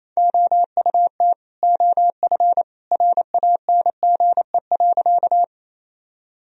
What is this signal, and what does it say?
Morse code, heard as a keyed tone, about 28 wpm, 700 Hz. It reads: OUT OF RANGE.
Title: Sentences from Top 1000 Words - Review Part 4 - 28wpm